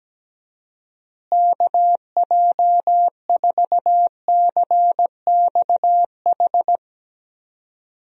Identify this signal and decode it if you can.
Morse code, heard as a keyed tone, about 17 wpm, 700 Hz. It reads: KJ4CXH
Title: USA Call Signs - 6 Characters - Repeated 3x - 17wpm